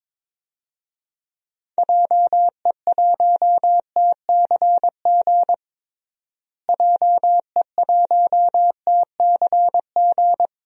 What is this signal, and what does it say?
Morse code, heard as a keyed tone, about 22 wpm, 700 Hz. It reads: JE1TCG JE1TCG
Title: DXCC-100 Call Signs - 6 Characters - Repeated 3x - 22wpm